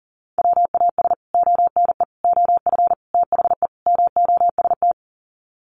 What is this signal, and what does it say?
Morse code, heard as a keyed tone, about 40 wpm, 700 Hz. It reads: WAS ONE OF THE MOST
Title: Sets of 5 Words 40wpm